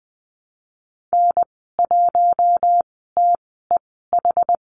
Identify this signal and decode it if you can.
Morse code, heard as a keyed tone, about 20 wpm, spaced as for 14 wpm, 700 Hz. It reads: N1TEH
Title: 076 Lesson 13g - Letter L - Call Signs 20wpm